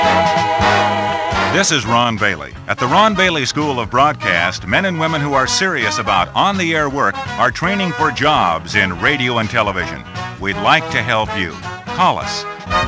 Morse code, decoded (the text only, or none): none